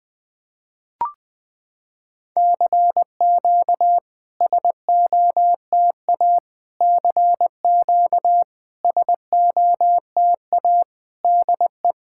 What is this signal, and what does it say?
Morse code, heard as a keyed tone, about 20 wpm, 700 Hz. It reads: CQ SOTA CQ SOTA DE
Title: SOTA QSO Practice 20wpm